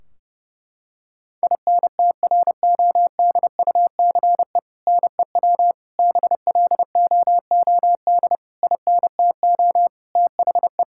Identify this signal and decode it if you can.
Morse code, heard as a keyed tone, about 30 wpm, 700 Hz. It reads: INTRODUCE NEW BLOOD INTO THE
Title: Sentences from Top 1600 Words 30wpm